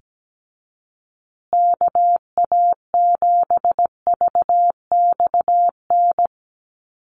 Morse code, heard as a keyed tone, about 17 wpm, 700 Hz. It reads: KA7VXN
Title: USA Call Signs Repeated 3x 17wpm